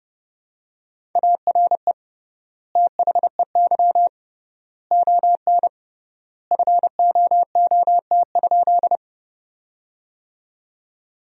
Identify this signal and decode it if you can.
Morse code, heard as a keyed tone, about 30 wpm, 700 Hz. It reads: ARE THEY ON FOOT?
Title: 3x Word-Spacing - Sentences from Top 500 Words - 30wpm